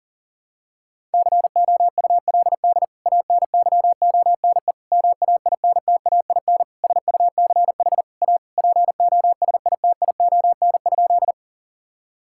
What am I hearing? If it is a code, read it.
Morse code, heard as a keyed tone, about 40 wpm, 700 Hz. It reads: COULD ANYONE MAINTAIN SUCH A POSITION?
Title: Sentences from Top 1000 Words - Review Part 5 - 40wpm